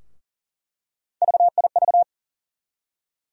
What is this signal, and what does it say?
Morse code, heard as a keyed tone, about 40 wpm, 700 Hz. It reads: VIV
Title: USA Call Sign Postfixes - 3 Letters 40wpm